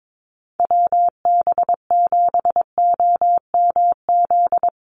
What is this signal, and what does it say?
Morse code, heard as a keyed tone, about 22 wpm, 700 Hz. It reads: WB7OMZ
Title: USA Call Signs Repeated 2x 22wpm